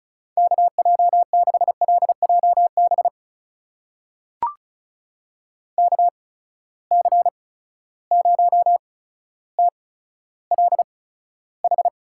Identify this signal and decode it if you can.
Morse code, heard as a keyed tone, about 35 wpm, spaced as for 15 wpm, 700 Hz. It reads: KJ6LJB KC0TLH
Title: USA Call Signs - 6 Characters - 8x Character Spacing - 35wpm